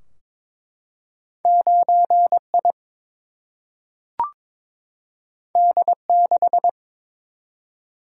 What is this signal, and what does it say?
Morse code, heard as a keyed tone, about 22 wpm, 700 Hz. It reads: 9 I D 6
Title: Single-Letter Single-Number 22wpm